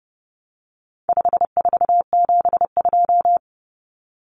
Morse code, heard as a keyed tone, about 30 wpm, 700 Hz. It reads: 5472
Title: Common Cut Numbers - 4-Digit 30wpm